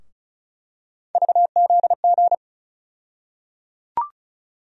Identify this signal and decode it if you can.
Morse code, heard as a keyed tone, about 36 wpm, 700 Hz. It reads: VZG
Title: USA Call Sign Postfixes - 3 Letters 35wpm